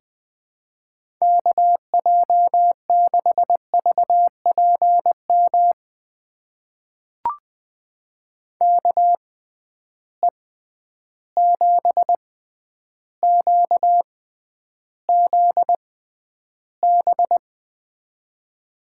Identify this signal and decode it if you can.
Morse code, heard as a keyed tone, about 20 wpm, spaced as for 10 wpm, 700 Hz. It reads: KJ6VPM KE7QZB
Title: USA Call Signs - 6 Characters - 6x Character Spacing - 20wpm